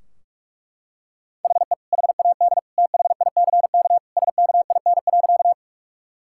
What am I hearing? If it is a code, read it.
Morse code, heard as a keyed tone, about 41 wpm, 700 Hz. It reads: HE HAD THICK SKIN.
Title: Sentences from Top 1600 Words 45wpm